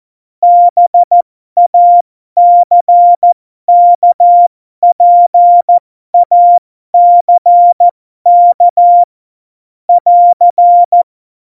Morse code, heard as a keyed tone, about 14 wpm, spaced as for 12 wpm, 700 Hz. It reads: BACKPACK <AR>